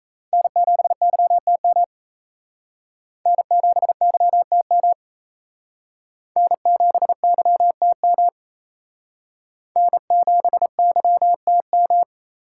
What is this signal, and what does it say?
Morse code, heard as a keyed tone, about 35 wpm, 700 Hz. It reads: N7YTM N7YTM N7YTM N7YTM
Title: Speed Racing - USA 1x3 Call Signs 28wpm